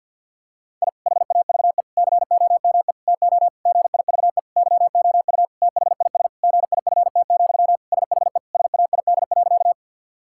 Morse code, heard as a keyed tone, about 50 wpm, 700 Hz. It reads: I HAVE COME TO GIVE YOU THIS GIFT, SHE SAID.
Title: Sentences from Top 1000 Words - Review Part 6 - 50wpm